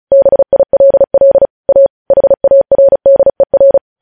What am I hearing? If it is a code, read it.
Morse code, heard as a keyed tone, about 36 wpm, 550 Hz. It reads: BILL A HARDER
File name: podcast / News headlines-1x-35x35